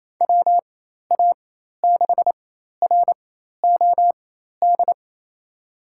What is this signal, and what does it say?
Morse code, heard as a keyed tone, about 28 wpm, spaced as for 13 wpm, 700 Hz. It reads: WA6ROD
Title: USA Call Signs - 4x Character Spacing - 28wpm